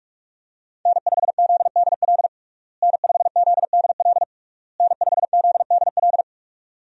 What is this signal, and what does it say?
Morse code, heard as a keyed tone, about 41 wpm, 700 Hz. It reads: N5ZDL N5ZDL N5ZDL
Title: USA Call Signs - 5 Characters - Repeated 3x - 45wpm